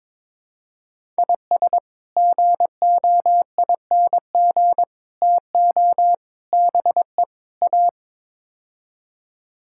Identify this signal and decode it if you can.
Morse code, heard as a keyed tone, about 21 wpm, 700 Hz. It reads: IS GOING TO BE A
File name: practice / 22wpm / Build A Phrase - 5-Word Phrases - 22wpm